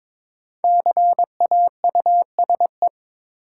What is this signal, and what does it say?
Morse code, heard as a keyed tone, about 22 wpm, 700 Hz. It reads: CAUSE
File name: 5 Letter Words - First 3 Letters Given - 22wpm